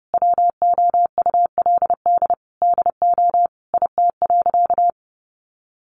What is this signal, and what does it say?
Morse code, heard as a keyed tone, about 30 wpm, 700 Hz. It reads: WOULD DO IT.
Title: Sentences from Top 1400 Words 30wpm